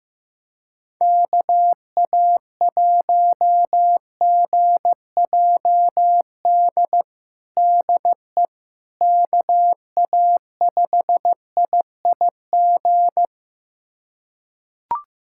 Morse code, heard as a keyed tone, about 15 wpm, 700 Hz. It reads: KA1GJD DE KA5IIG
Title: USA Callsign DE Callsign - 6 Characters - 15wpm